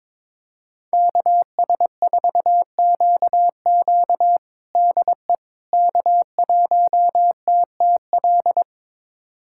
Morse code, heard as a keyed tone, about 21 wpm, 700 Hz. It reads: KS4QQ DE K1TTL
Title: USA Callsign DE Callsign - 5 Characters - 22wpm